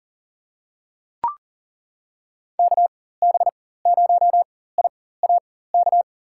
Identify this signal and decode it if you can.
Morse code, heard as a keyed tone, about 40 wpm, spaced as for 19 wpm, 700 Hz. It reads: KB0IAK
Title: USA Call Signs - 6 Characters - 4x Character Spacing - 40wpm